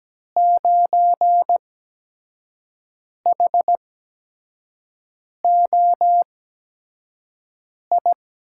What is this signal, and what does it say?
Morse code, heard as a keyed tone, about 17 wpm, spaced as for 5 wpm, 700 Hz. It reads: 9HOI
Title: USA Call Signs - 8x Character Spacing - 17wpm